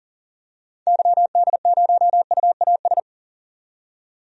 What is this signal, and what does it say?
Morse code, heard as a keyed tone, about 38 wpm, 700 Hz. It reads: YD0UAS
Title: DXCC-100 Call Signs - 6 Characters - 40wpm